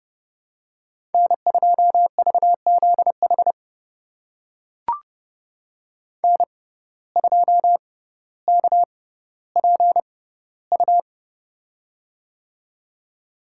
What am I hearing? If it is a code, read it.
Morse code, heard as a keyed tone, about 31 wpm, spaced as for 16 wpm, 700 Hz. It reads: N2VZH N2KPU
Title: USA Call Signs - 6x Character Spacing - 30wpm